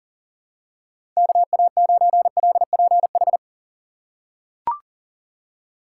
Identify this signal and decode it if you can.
Morse code, heard as a keyed tone, about 37 wpm, 700 Hz. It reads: KA9LPH
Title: USA Call Signs - 3x Character Spacing - 40wpm